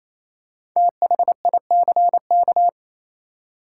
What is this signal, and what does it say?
Morse code, heard as a keyed tone, about 28 wpm, 700 Hz. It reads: THICK
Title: Five Letter Words - 28wpm